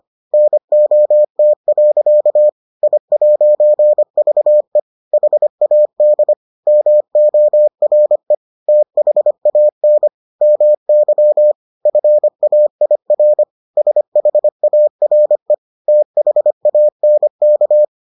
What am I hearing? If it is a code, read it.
Morse code, heard as a keyed tone, about 25 wpm, 600 Hz. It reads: NOT. I'VE HAD MORE THAN MY FAIR SHARE THANK